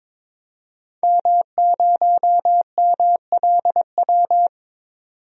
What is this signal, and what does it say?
Morse code, heard as a keyed tone, about 22 wpm, 700 Hz. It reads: M0MLW
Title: DXCC-100 Call Signs - 5 Characters - 22wpm